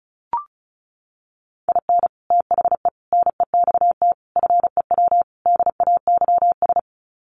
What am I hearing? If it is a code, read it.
Morse code, heard as a keyed tone, about 34 wpm, 700 Hz. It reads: IN THE NEXT FEW DAYS
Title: Sets of 5 Words Encore2 35wpm